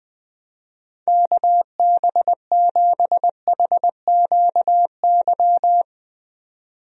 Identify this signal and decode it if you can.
Morse code, heard as a keyed tone, about 21 wpm, 700 Hz. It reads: KB7HQY
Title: USA Call Signs - 6 Characters - 20wpm